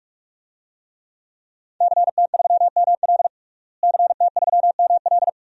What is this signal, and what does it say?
Morse code, heard as a keyed tone, about 46 wpm, 700 Hz. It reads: CT3ML CT3ML
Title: DXCC-100 Call Signs - 5 Characters - Repeated 3x - 45wpm